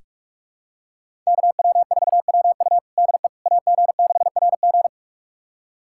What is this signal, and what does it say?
Morse code, heard as a keyed tone, about 46 wpm, 700 Hz. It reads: KW4WU DE AG6RG
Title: USA Callsign DE Callsign - 5 Characters - 45wpm